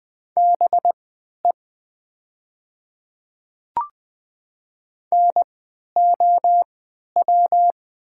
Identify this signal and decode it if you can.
Morse code, heard as a keyed tone, about 20 wpm, spaced as for 11 wpm, 700 Hz. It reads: BE NOW
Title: 3x Character Spacing - Top 100 Words - 20wpm